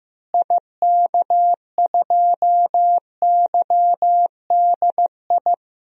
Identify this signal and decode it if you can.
Morse code, heard as a keyed tone, about 15 wpm, 700 Hz. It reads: IK2YDI
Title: DXCC-100 Call Signs - 6 Characters - Repeated 3x - 15wpm